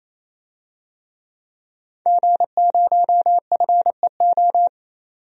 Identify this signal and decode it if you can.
Morse code, heard as a keyed tone, about 28 wpm, 700 Hz. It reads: G0FEO
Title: DXCC-100 Call Signs - 5 Characters - 28wpm